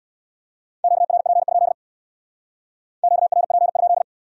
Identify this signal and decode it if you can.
Morse code, heard as a keyed tone, about 66 wpm, 700 Hz. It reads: QRP? QRP?